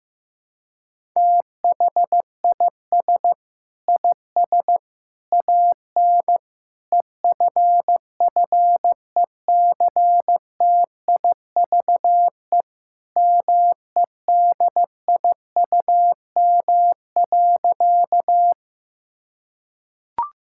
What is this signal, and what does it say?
Morse code, heard as a keyed tone, about 15 wpm, 700 Hz. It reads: THIS IS AN EFFECTIVE MEDIUM.
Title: Sentences from Top 1000 Words - Review Part 6 - 15wpm